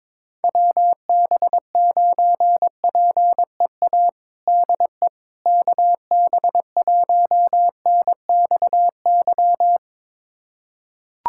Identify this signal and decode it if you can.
Morse code, heard as a keyed tone, about 22 wpm, 700 Hz. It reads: WB9PEA DE KB1NXY
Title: USA Callsign DE Callsign - 6 Characters - 22wpm